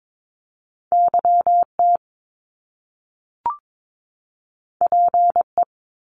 Morse code, heard as a keyed tone, about 22 wpm, 700 Hz. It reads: YT PE
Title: Canadian Provinces and Territories 22wpm